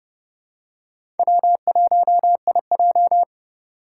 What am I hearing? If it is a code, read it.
Morse code, heard as a keyed tone, about 31 wpm, 700 Hz. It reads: W1IJ
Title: USA Call Signs - 2x Character Spacing - 30wpm